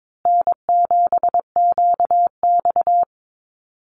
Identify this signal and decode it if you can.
Morse code, heard as a keyed tone, about 22 wpm, 700 Hz. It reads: N7QX
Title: USA Call Signs Repeated 3x 22wpm